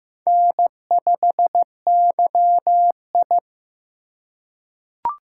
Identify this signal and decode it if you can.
Morse code, heard as a keyed tone, about 15 wpm, 700 Hz. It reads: N5YI
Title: USA Call Signs - 4 Characters - 8x Character Spacing - 15wpm